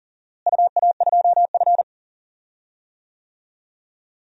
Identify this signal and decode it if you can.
Morse code, heard as a keyed tone, about 40 wpm, 700 Hz. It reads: UA2F
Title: DXCC-100 Call Signs - 4 Characters - Repeated 3x - 40wpm